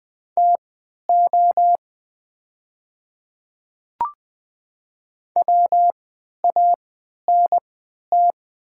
Morse code, heard as a keyed tone, about 21 wpm, spaced as for 12 wpm, 700 Hz. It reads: TO WANT
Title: 3x Character Spacing - Top 200 Words - 20wpm